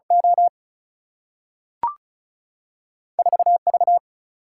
Morse code, heard as a keyed tone, about 36 wpm, 700 Hz. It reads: O 4 V
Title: Single-Letter Single-Number 35wpm